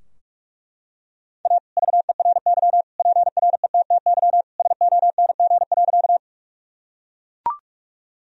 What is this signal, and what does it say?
Morse code, heard as a keyed tone, about 41 wpm, 700 Hz. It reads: A VERY PRETTY SONG.
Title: Sentences from Top 1000 Words - Review Part 2 - 45wpm